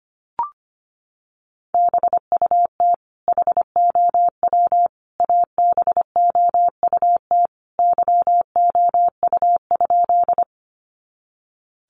Morse code, heard as a keyed tone, about 25 wpm, 700 Hz. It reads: BUT HOW ABOUT YOU?
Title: Sentences from Top 100 Words 25wpm